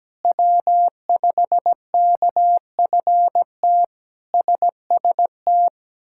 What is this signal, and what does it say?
Morse code, heard as a keyed tone, about 17 wpm, 700 Hz. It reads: W5KFT SST
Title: K1USN SST QSO Practice - 17wpm